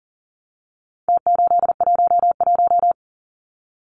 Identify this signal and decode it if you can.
Morse code, heard as a keyed tone, about 40 wpm, 700 Hz. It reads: T811
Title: Numbers - Contest Style - 4-Digit 40wpm